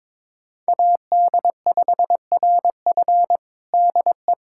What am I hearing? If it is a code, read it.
Morse code, heard as a keyed tone, about 22 wpm, 700 Hz. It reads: AD5RF DE
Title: USA Callsign DE Callsign - 5 Characters - 22wpm